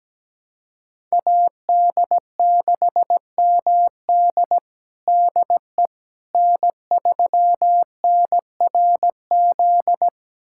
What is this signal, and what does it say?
Morse code, heard as a keyed tone, about 17 wpm, 700 Hz. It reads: AD6MD DE N3NRZ
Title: USA Callsign DE Callsign - 5 Characters - 17wpm